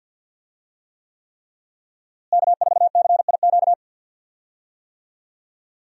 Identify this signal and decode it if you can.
Morse code, heard as a keyed tone, about 50 wpm, 700 Hz. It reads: K4CIQ
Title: USA Call Signs - 5 Characters - Repeated 3x - 50wpm